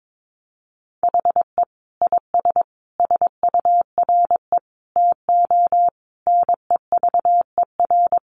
Morse code, heard as a keyed tone, about 22 wpm, 700 Hz. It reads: HE IS SURE TO NEVER
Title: Sentences from Top 700 Words 22wpm